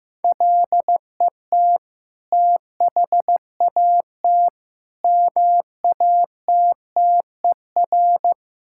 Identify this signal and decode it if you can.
Morse code, heard as a keyed tone, about 15 wpm, 690 Hz. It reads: LET THAT MATTER
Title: Sentences from Top 1000 Words - Review Part 5 - 15wpm